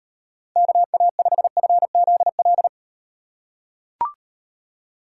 Morse code, heard as a keyed tone, about 38 wpm, 700 Hz. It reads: KA5FZL